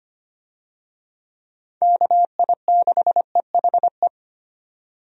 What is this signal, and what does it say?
Morse code, heard as a keyed tone, about 25 wpm, 700 Hz. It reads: KI6EHE
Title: USA Call Signs - 6 Characters - 25wpm